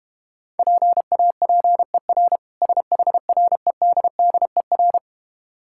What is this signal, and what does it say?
Morse code, heard as a keyed tone, about 32 wpm, 700 Hz. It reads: PAPER SHREDDER